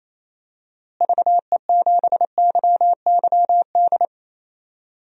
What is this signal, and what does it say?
Morse code, heard as a keyed tone, about 28 wpm, 700 Hz. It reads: VE7YYD
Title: DXCC-100 Call Signs - 6 Characters - Repeated 2x - 28wpm